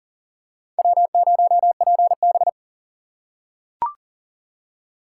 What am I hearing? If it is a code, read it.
Morse code, heard as a keyed tone, about 39 wpm, 700 Hz. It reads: W0PB
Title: USA Call Signs - 4 Characters - 6x Character Spacing - 40wpm